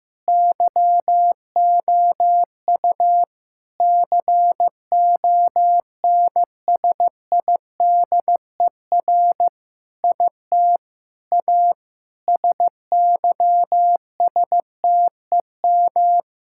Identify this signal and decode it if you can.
Morse code, heard as a keyed tone, about 15 wpm, 700 Hz. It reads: YOU CONSIDER IT A SYSTEM
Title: Sentences from Top 1000 Words - Review Part 6 - 15wpm